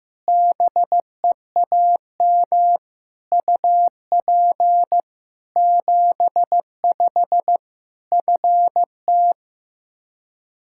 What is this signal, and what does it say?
Morse code, heard as a keyed tone, about 15 wpm, 700 Hz. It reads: BEAM UP 75 FT
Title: Practice QSO - Exchange 2 - Info - 15wpm